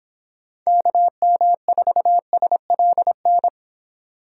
Morse code, heard as a keyed tone, about 26 wpm, 700 Hz. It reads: KM4SLN